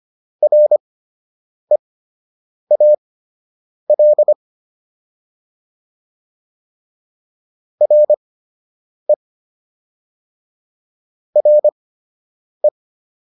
Morse code, heard as a keyed tone, about 25 wpm, spaced as for 8 wpm, 600 Hz. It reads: REAL RE RE